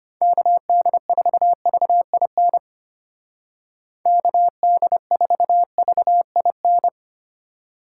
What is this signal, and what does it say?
Morse code, heard as a keyed tone, about 31 wpm, 700 Hz. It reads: KD4VIN KD4VIN
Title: Speed Racing - USA 2x3 Call Signs 25wpm